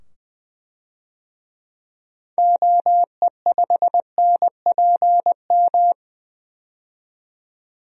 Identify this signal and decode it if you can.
Morse code, heard as a keyed tone, about 21 wpm, 700 Hz. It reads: OE5NPM
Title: DXCC-100 Call Signs - 6 Characters - 20wpm